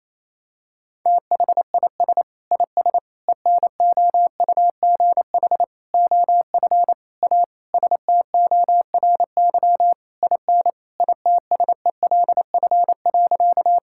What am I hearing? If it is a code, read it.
Morse code, heard as a keyed tone, about 28 wpm, 700 Hz. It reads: THIS IS ENOUGH OF A STORY IN ITSELF.
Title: Sentences from Top 1000 Words - Review Part 5 - 28wpm